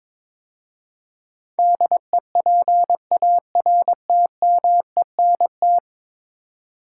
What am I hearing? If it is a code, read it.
Morse code, heard as a keyed tone, about 22 wpm, 700 Hz. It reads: DEPARTMENT
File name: practice / 22wpm / 10 Letter Words - First 3 Letters Given - 22wpm